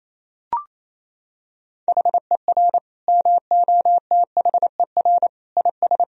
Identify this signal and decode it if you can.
Morse code, heard as a keyed tone, about 28 wpm, 700 Hz. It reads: HER MOTHER IS
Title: Sentences from Top 1000 Words - Review Part 3 - 28wpm